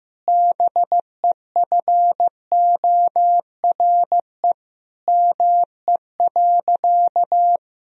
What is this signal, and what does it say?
Morse code, heard as a keyed tone, about 16 wpm, 700 Hz. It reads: BEFORE ME.
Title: Sentences from Top 1000 Words - Review Part 1 - 15wpm